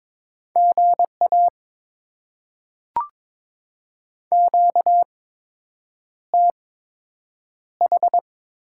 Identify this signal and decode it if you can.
Morse code, heard as a keyed tone, about 22 wpm, spaced as for 8 wpm, 700 Hz. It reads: GA QTH
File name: Top 20 QSO Elements - 8x Character Spacing - 22wpm